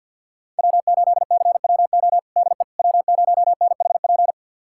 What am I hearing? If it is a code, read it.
Morse code, heard as a keyed tone, about 46 wpm, 700 Hz. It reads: W8CWO DE W0NHP
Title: USA Callsign DE Callsign - 5 Characters - 50wpm